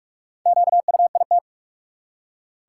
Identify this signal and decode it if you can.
Morse code, heard as a keyed tone, about 42 wpm, 700 Hz. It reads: QUIT
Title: Four Letter Words - 45wpm